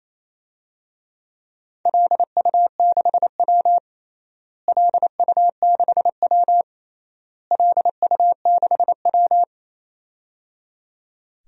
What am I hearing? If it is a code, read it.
Morse code, heard as a keyed tone, about 28 wpm, 700 Hz. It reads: LU6W LU6W LU6W
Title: DXCC-100 Call Signs - 4 Characters - Repeated 3x - 28wpm